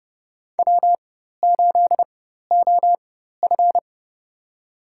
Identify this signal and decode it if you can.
Morse code, heard as a keyed tone, about 30 wpm, spaced as for 14 wpm, 700 Hz. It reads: W8OF
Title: USA Call Signs - 4 Characters - 4x Character Spacing - 30wpm